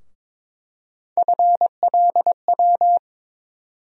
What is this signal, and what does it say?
Morse code, heard as a keyed tone, about 22 wpm, 700 Hz. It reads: FLW
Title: USA Call Sign Postfixes - 3 Letters 22wpm